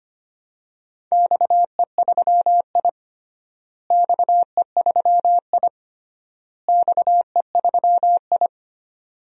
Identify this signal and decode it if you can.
Morse code, heard as a keyed tone, about 25 wpm, 700 Hz. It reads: XE3I XE3I XE3I
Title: DXCC-100 Call Signs - 4 Characters - Repeated 3x - 25wpm